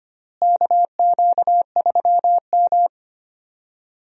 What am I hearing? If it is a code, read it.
Morse code, heard as a keyed tone, about 25 wpm, 700 Hz. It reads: KQ3M
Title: USA Call Signs - 4 Characters - Repeated 3x - 25wpm